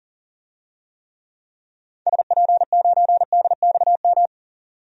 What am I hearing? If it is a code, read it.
Morse code, heard as a keyed tone, about 40 wpm, 700 Hz. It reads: SP9DXM
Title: DXCC-100 Call Signs - 6 Characters - 40wpm